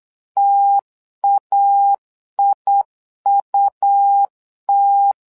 Morse code, heard as a keyed tone, about 8 wpm, 800 Hz. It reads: TAIUT
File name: LEZIONI / 2 UHDN / 2 UHDN 8wpm2232